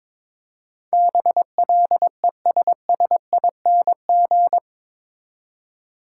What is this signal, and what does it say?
Morse code, heard as a keyed tone, about 22 wpm, 700 Hz. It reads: BLESSING
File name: practice / 22wpm / Eight Letter Words - 22wpm